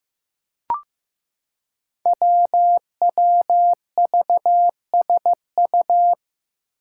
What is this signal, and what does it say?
Morse code, heard as a keyed tone, about 15 wpm, 700 Hz. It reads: WWVSU
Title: Five Challenging Letter-Number Combinations 15wpm